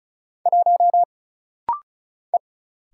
Morse code, heard as a keyed tone, about 36 wpm, 700 Hz. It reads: 1 E
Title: Single Letter-Number - Rapid-Fire 35wpm